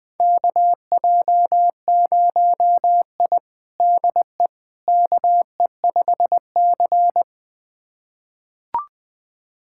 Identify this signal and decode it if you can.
Morse code, heard as a keyed tone, about 21 wpm, 700 Hz. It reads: KJ0I DE KE5C
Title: USA Callsign DE Callsign - 4 Characters - 20wpm